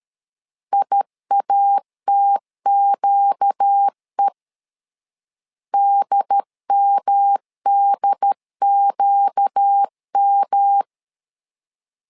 Morse code, heard as a keyed tone, about 13 wpm, 800 Hz. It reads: IATQE DMDQM
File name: LEZIONI / 4 Q 12wpm 1833